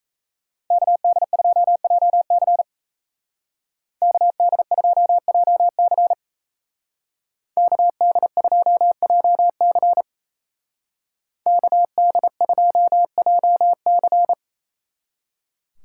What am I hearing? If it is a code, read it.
Morse code, heard as a keyed tone, about 36 wpm, 700 Hz. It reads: KD2JC KD2JC KD2JC KD2JC